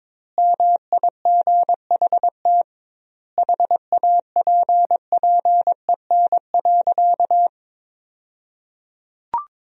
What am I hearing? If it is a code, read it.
Morse code, heard as a keyed tone, about 22 wpm, 700 Hz. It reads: MIGHT HAPPEN.
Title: 2x Word-Spacing - Sentences from Top 400 Words - 22wpm